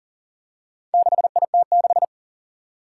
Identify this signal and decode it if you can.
Morse code, heard as a keyed tone, about 40 wpm, 700 Hz. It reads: 6IT6
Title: Four Challenging Letter-Number Combinations 40wpm